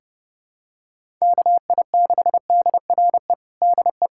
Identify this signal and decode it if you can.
Morse code, heard as a keyed tone, about 32 wpm, 700 Hz. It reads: KI6DRE DE
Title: USA Callsign DE Callsign 30wpm